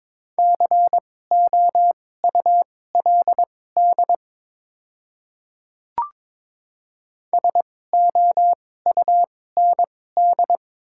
Morse code, heard as a keyed tone, about 22 wpm, spaced as for 16 wpm, 700 Hz. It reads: COULD SOUND
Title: 2x Character Spacing - Top 100 Words - 22wpm